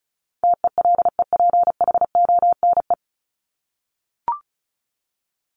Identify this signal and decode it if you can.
Morse code, heard as a keyed tone, about 34 wpm, 700 Hz. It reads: TELEPHONE
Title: Top 1501-1750 Words 35wpm